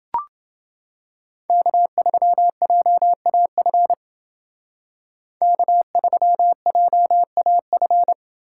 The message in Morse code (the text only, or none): K3JAF K3JAF